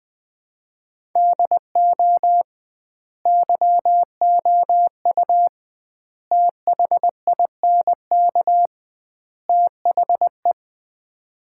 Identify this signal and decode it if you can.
Morse code, heard as a keyed tone, about 20 wpm, 700 Hz. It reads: DO YOU THINK THE
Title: 2x Word-Spacing - Sets of 4 Words - 20wpm